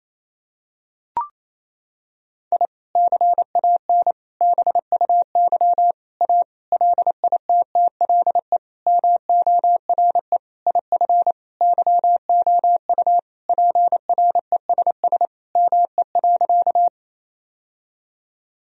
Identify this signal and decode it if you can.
Morse code, heard as a keyed tone, about 28 wpm, 700 Hz. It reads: I CAN BUY A LITTLE MORE IF YOU PRESS ME.
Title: Sentences from Top 1000 Words - Review Part 1 - 28wpm